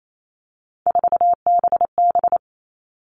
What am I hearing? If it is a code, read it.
Morse code, heard as a keyed tone, about 28 wpm, 700 Hz. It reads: 4BB
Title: All Cut Numbers - 3-Digit 28wpm